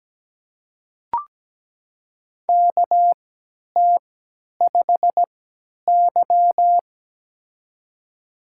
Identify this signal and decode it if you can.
Morse code, heard as a keyed tone, about 17 wpm, spaced as for 10 wpm, 700 Hz. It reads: KT5Y
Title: USA Call Signs - 4 Characters - 3x Character Spacing - 17wpm